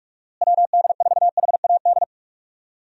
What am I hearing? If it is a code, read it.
Morse code, heard as a keyed tone, about 42 wpm, 700 Hz. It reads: WD4HAD
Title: USA Call Signs - 6 Characters - Repeated 3x - 45wpm